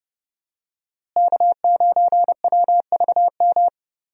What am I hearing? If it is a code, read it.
Morse code, heard as a keyed tone, about 32 wpm, 700 Hz. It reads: K9WVM
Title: USA Call Signs - 5 Characters - Repeated 3x - 30wpm